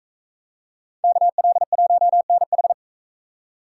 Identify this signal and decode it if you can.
Morse code, heard as a keyed tone, about 38 wpm, 700 Hz. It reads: KL1NH